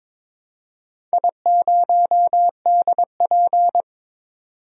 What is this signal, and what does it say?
Morse code, heard as a keyed tone, about 22 wpm, 700 Hz. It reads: I0DP
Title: DXCC-100 Call Signs - 4 Characters - 22wpm